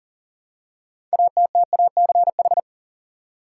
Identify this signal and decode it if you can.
Morse code, heard as a keyed tone, about 40 wpm, 700 Hz. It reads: ATTACH
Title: Six Letter Words - 40wpm